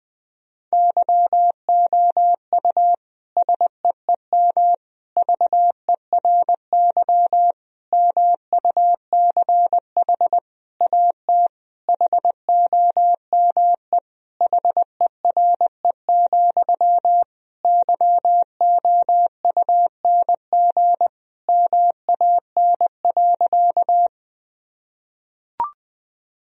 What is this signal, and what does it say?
Morse code, heard as a keyed tone, about 20 wpm, 700 Hz. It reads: YOU SEEM VERY MUCH AT HOME HERE, YOUNG MAN.
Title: Sentences from Top 1000 Words - Review Part 1 - 20wpm